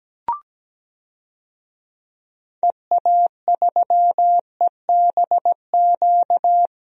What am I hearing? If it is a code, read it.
Morse code, heard as a keyed tone, about 17 wpm, 700 Hz. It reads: EA3EBQ
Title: DXCC-100 Call Signs - 6 Characters - 17wpm